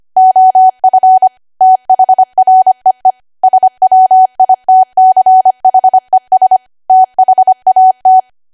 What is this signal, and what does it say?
Morse code, heard as a keyed tone, about 25 wpm, 750 Hz. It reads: OF THREE SWITCHES THAT
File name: cwa / W1AW-Level3 / 25WPM